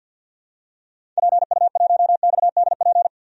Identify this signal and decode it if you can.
Morse code, heard as a keyed tone, about 52 wpm, 700 Hz. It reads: PU1XDP
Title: DXCC-100 Call Signs - 6 Characters - 50wpm